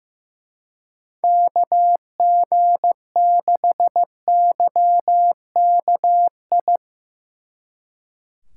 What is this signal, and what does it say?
Morse code, heard as a keyed tone, about 17 wpm, 700 Hz. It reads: KG6YKI
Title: Speed Racing - USA 2x3 Call Signs 15wpm